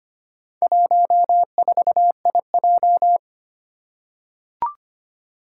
Morse code, heard as a keyed tone, about 25 wpm, 700 Hz. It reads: 14IJ
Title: Four Challenging Letter-Number Combinations 25wpm